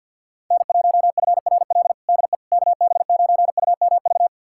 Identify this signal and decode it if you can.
Morse code, heard as a keyed tone, about 51 wpm, 700 Hz. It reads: N1FRL DE KB9UMV
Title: USA Callsign DE Callsign 50wpm